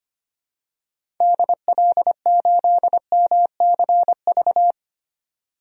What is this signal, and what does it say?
Morse code, heard as a keyed tone, about 25 wpm, 700 Hz. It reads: DL8MCV
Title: DXCC-100 Call Signs - 6 Characters - 25wpm